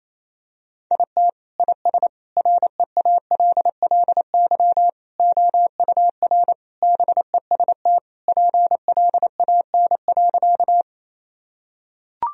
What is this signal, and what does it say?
Morse code, heard as a keyed tone, about 28 wpm, 700 Hz. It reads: IT IS REALLY OUR BEST PLAN.
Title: Sentences from Top 1000 Words - Review Part 2 - 28wpm